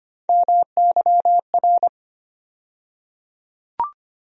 Morse code, heard as a keyed tone, about 25 wpm, 700 Hz. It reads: MYR
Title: USA Call Sign Postfixes - 3 Letters 25wpm